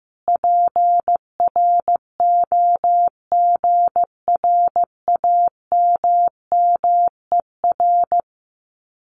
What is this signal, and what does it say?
Morse code, heard as a keyed tone, about 15 wpm, 700 Hz. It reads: PROGRAMMER
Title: Common QSO Phrases Encore 15wpm